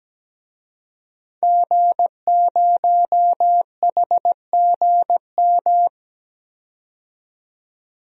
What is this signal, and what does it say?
Morse code, heard as a keyed tone, about 17 wpm, 700 Hz. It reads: G0HGM
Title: DXCC-100 Call Signs - 5 Characters - 17wpm